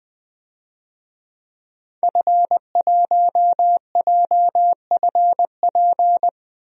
Morse code, heard as a keyed tone, about 21 wpm, 700 Hz. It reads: F1JFP
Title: DXCC-100 Call Signs - 5 Characters - Repeated 2x - 20wpm